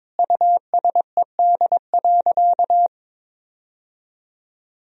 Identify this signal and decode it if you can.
Morse code, heard as a keyed tone, about 21 wpm, 700 Hz. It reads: USED.
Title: Sentences from Top 1000 Words - Review Part 1 - 22wpm